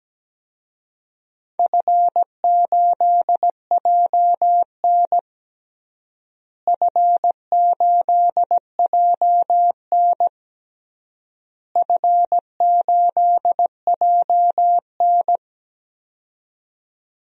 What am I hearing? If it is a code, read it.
Morse code, heard as a keyed tone, about 17 wpm, 700 Hz. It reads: F8JN F8JN F8JN
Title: DXCC-100 Call Signs - 4 Characters - Repeated 3x - 17wpm